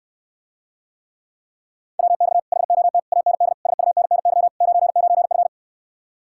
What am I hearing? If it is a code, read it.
Morse code, heard as a keyed tone, about 69 wpm, 700 Hz. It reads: WX HOT AND SUNNY 99F